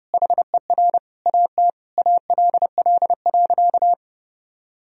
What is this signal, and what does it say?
Morse code, heard as a keyed tone, about 31 wpm, 700 Hz. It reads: HER AT ALL.
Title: Sentences from Top 1000 Words - Review Part 2 - 30wpm